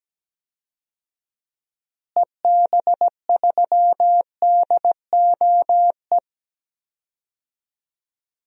Morse code, heard as a keyed tone, about 17 wpm, 700 Hz. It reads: EB3DOE
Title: DXCC-100 Call Signs - 6 Characters - Repeated 3x - 17wpm